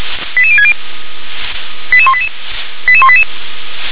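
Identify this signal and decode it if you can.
Five-tone selective call (ZVEI), heard as quick runs of five tones, tones of about 70 ms. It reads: 88060 88188 88188
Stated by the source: ZVEI five-tone selective call 88105, received on SDR.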